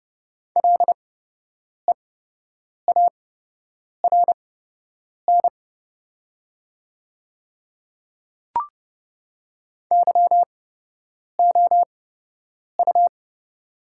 Morse code, heard as a keyed tone, about 30 wpm, spaced as for 8 wpm, 700 Hz. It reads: LEARN YOU
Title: 8x Character Spacing - Top 300 Words - 30wpm